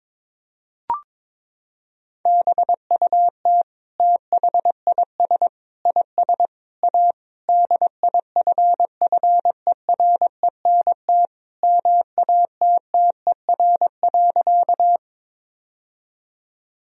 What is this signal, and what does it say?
Morse code, heard as a keyed tone, about 22 wpm, 700 Hz. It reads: BUT THIS IS A DIFFERENT MATTER.
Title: Sentences from Top 1000 Words - Review Part 3 - 22wpm